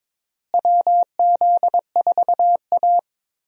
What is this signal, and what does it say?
Morse code, heard as a keyed tone, about 22 wpm, 700 Hz. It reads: WZ4A
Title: USA Call Signs - 4 Characters - Repeated 3x - 22wpm